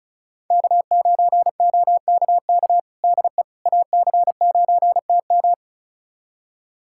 Morse code, heard as a keyed tone, about 34 wpm, 700 Hz. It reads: K9OKK DE AC9TM
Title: USA Callsign DE Callsign - 5 Characters - 35wpm